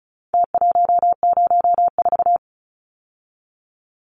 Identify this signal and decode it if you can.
Morse code, heard as a keyed tone, about 34 wpm, 700 Hz. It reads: T104
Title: Numbers - Contest Style - 4-Digit 35wpm